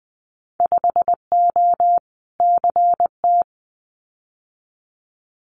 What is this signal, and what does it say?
Morse code, heard as a keyed tone, about 20 wpm, 700 Hz. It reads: 5O CT
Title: Winter Field Day Exchanges Encore 20wpm